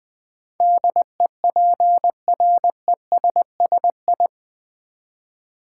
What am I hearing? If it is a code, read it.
Morse code, heard as a keyed tone, about 20 wpm, 700 Hz. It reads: DEPRESSI
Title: Build A Word - 4 Letters At A Time - 10-Letter Words - 20wpm